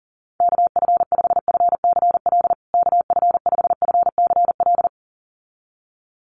Morse code, heard as a keyed tone, about 40 wpm, 700 Hz. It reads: KF5FCL KF5FCL
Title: USA Call Signs Repeated 2x 40wpm